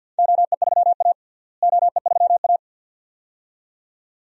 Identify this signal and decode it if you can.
Morse code, heard as a keyed tone, about 51 wpm, 700 Hz. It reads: OE3A OE3A
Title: DXCC-100 Call Signs - 4 Characters - Repeated 2x - 50wpm